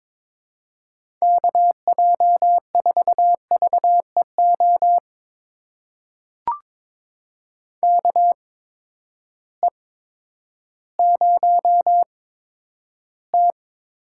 Morse code, heard as a keyed tone, about 22 wpm, spaced as for 11 wpm, 700 Hz. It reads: KJ4VEO KE0T
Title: USA Call Signs - 6 Characters - 8x Character Spacing - 22wpm